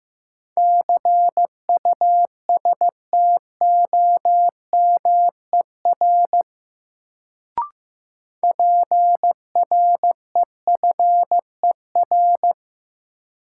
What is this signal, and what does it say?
Morse code, heard as a keyed tone, about 15 wpm, 700 Hz. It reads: CUSTOMER PREFER
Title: Sentences from Top 1100 Words - New Word Drill Alt - 15wpm